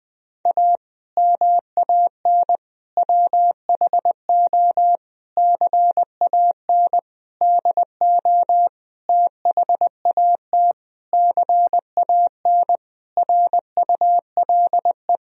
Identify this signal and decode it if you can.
Morse code, heard as a keyed tone, about 20 wpm, 700 Hz. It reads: A MAN WHO CAN DO THAT CAN RULE
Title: Sentences from Top 1000 Words - Review Part 2 - 20wpm